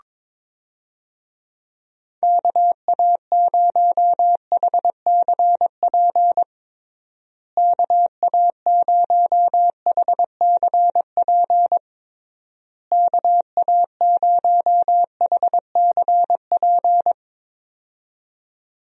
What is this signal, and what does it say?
Morse code, heard as a keyed tone, about 22 wpm, 700 Hz. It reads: KA0HCP KA0HCP KA0HCP
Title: USA Call Signs - 6 Characters - Repeated 3x - 22wpm